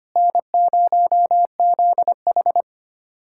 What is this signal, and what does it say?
Morse code, heard as a keyed tone, about 25 wpm, 700 Hz. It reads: N0ZH
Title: USA Callsign DE Callsign - 4 Characters - 25wpm